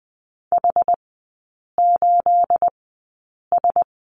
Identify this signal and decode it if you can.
Morse code, heard as a keyed tone, about 20 wpm, spaced as for 8 wpm, 690 Hz. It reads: H8S
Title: 212 Lesson 34e - Number 8 - Three Characters 20wpm